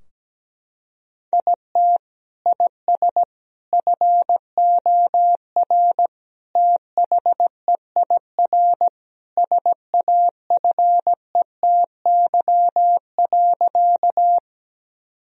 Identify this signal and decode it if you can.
Morse code, heard as a keyed tone, about 17 wpm, 700 Hz. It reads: IT IS FOR THEIR SAFETY.
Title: Sentences from Top 1000 Words - Review Part 4 - 17wpm